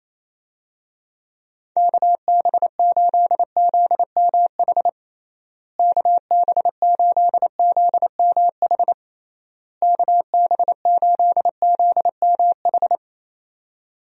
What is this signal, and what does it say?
Morse code, heard as a keyed tone, about 28 wpm, 700 Hz. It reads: KB8ZMH KB8ZMH KB8ZMH
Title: USA Call Signs - 6 Characters - Repeated 3x - 28wpm